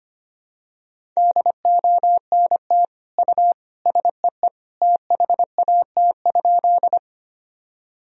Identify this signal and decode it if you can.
Morse code, heard as a keyed tone, about 25 wpm, 700 Hz. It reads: DONT U SEE THAT?
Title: Sentences from Top 300 Words and Abbreviations 25wpm